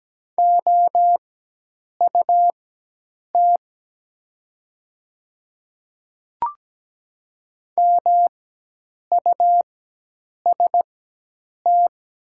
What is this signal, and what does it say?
Morse code, heard as a keyed tone, about 17 wpm, spaced as for 8 wpm, 700 Hz. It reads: OUT MUST
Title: 4x Character Spacing - Top 200 Words - 17wpm